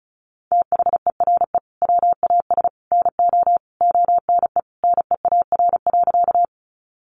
Morse code, heard as a keyed tone, about 34 wpm, 700 Hz. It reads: THERE WAS NO ONE NEAR.
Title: Sentences from Top 200 Words 35wpm